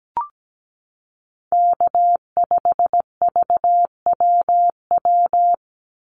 Morse code, heard as a keyed tone, about 17 wpm, 700 Hz. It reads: K5VWW
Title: Call Signs Repeated 2x - 17wpm